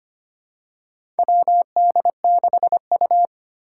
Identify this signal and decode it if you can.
Morse code, heard as a keyed tone, about 25 wpm, 700 Hz. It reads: WD6U
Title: USA Call Signs - 4 Characters - 3x Character Spacing - 25wpm